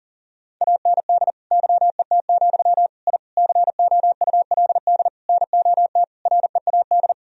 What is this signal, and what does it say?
Morse code, heard as a keyed tone, about 40 wpm, 700 Hz. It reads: AND YET, I COULD NOT READ
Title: Sentences from Top 1000 Words - Review Part 1 - 40wpm